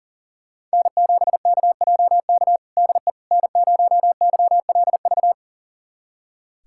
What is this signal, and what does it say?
Morse code, heard as a keyed tone, about 37 wpm, 700 Hz. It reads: N7KJK DE N0YLV
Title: USA Callsign DE Callsign - 5 Characters - 40wpm